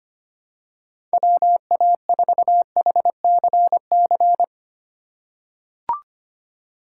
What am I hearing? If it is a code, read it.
Morse code, heard as a keyed tone, about 25 wpm, 700 Hz. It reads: WA4HCC
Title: K1USN SST QSO Practice - 25wpm